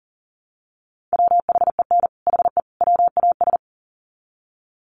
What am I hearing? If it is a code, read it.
Morse code, heard as a keyed tone, about 40 wpm, 700 Hz. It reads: WHEN HE WAS
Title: Sets of 3 Words 40wpm